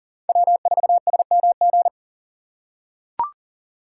Morse code, heard as a keyed tone, about 38 wpm, 700 Hz. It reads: W4SMG
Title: USA Call Signs - 5 Characters - 3x Character Spacing - 40wpm